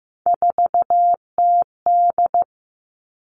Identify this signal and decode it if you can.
Morse code, heard as a keyed tone, about 15 wpm, 700 Hz. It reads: 4TD
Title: All Cut Numbers - 3-Digit 15wpm